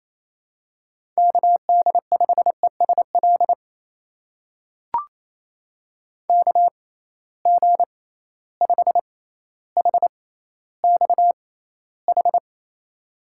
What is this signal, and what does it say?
Morse code, heard as a keyed tone, about 28 wpm, spaced as for 14 wpm, 700 Hz. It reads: KD5ESL KG5HXH
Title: USA Call Signs - 6x Character Spacing - 28wpm